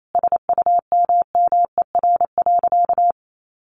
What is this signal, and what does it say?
Morse code, heard as a keyed tone, about 28 wpm, 700 Hz. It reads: SUMMER.
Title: Sentences from Top 1100 Words 28wpm